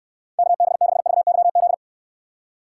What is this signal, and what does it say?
Morse code, heard as a keyed tone, about 66 wpm, 700 Hz. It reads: KB6VYZ